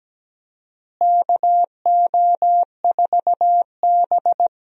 Morse code, heard as a keyed tone, about 17 wpm, 700 Hz. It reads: KO4B
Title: USA Call Signs - 4 Characters - Repeated 3x - 17wpm